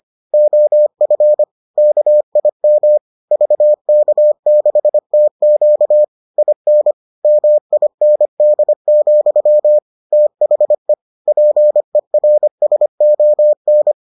Morse code, heard as a keyed tone, about 25 wpm, 600 Hz. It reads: OF KIM VK6TQ IN MIND, THE PERSON